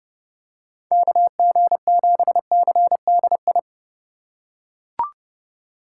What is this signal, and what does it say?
Morse code, heard as a keyed tone, about 30 wpm, 700 Hz. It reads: KG7CDI
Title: USA Call Signs - 6 Characters - 3x Character Spacing - 30wpm